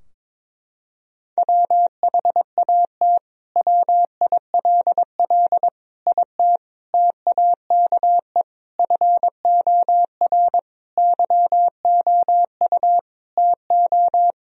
Morse code, heard as a keyed tone, about 21 wpm, 700 Hz. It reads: WHAT WILL IT TAKE FOR YOU TO
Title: Sentences from Top 1000 Words - Review Part 3 - 22wpm